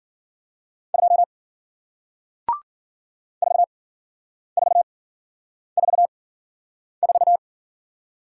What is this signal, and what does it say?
Morse code, heard as a keyed tone, about 53 wpm, spaced as for 9 wpm, 700 Hz. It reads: 2 4444